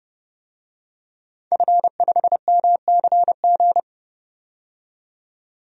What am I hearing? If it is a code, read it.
Morse code, heard as a keyed tone, about 30 wpm, 700 Hz. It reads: F5MCG
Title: DXCC-100 Call Signs - 5 Characters - 30wpm